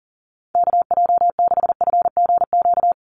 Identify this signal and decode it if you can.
Morse code, heard as a keyed tone, about 40 wpm, 700 Hz. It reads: KJ6FGQ
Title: USA Call Signs 40wpm